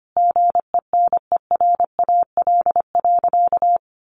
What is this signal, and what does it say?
Morse code, heard as a keyed tone, about 25 wpm, 700 Hz. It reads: GENERAL.
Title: Sentences from Top 1500 Words 25wpm